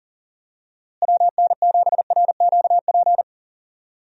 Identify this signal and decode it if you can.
Morse code, heard as a keyed tone, about 40 wpm, 700 Hz. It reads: WN7RQP
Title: USA Call Signs - 6 Characters - Repeated 2x - 40wpm